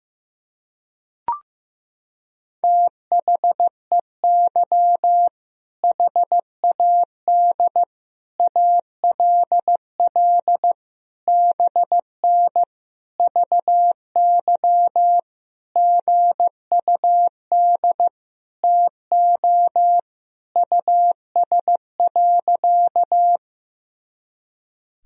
Morse code, heard as a keyed tone, about 15 wpm, 700 Hz. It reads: THEY HAD ALL BN VY GUD TO US.
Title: Sentences from Top 200 Words and Abbreviations 15wpm